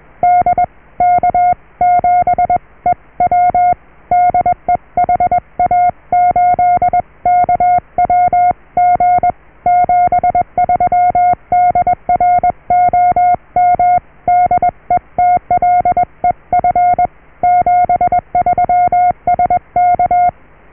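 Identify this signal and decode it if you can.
Morse code, heard as a keyed tone, about 21 wpm, 720 Hz. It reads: DK7EW DE HA8KWG 73 DR OM DETLEF 73 SK
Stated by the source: Morse QSO on 14 MHz with a German station, received at RST 579.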